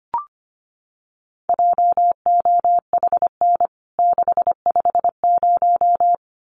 Morse code, heard as a keyed tone, about 25 wpm, 700 Hz. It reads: JOHN 650
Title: CWT Exchange 25wpm